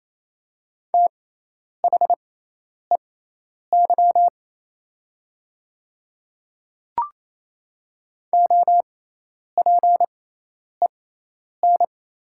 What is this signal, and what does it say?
Morse code, heard as a keyed tone, about 28 wpm, spaced as for 10 wpm, 700 Hz. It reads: THEY OPEN
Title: 6x Character Spacing - Top 300 Words - 28wpm